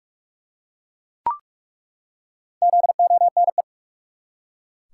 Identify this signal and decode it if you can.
Morse code, heard as a keyed tone, about 41 wpm, 700 Hz. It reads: ZONE
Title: Four Letter Words - 45wpm